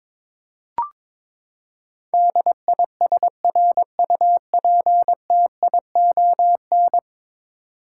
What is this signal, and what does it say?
Morse code, heard as a keyed tone, about 22 wpm, 700 Hz. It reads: DISRUPTION
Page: morse-code-ninja-practice-22wpm